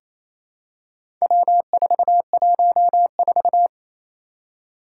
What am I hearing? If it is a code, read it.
Morse code, heard as a keyed tone, about 27 wpm, 700 Hz. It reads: W414
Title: Four Challenging Letter-Number Combinations 28wpm